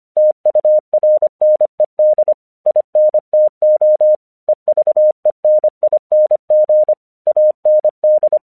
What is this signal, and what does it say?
Morse code, heard as a keyed tone, about 24 wpm, 610 Hz. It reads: TURNED INTO EVENING AND
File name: 1 English 25wpm / AQC18